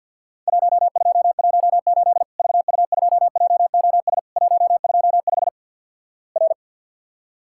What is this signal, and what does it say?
Morse code, heard as a keyed tone, about 48 wpm, 700 Hz. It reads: 1218 VU2JOS 125 R
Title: Worked All Europe DX Contest - QTC Exchange - 50wpm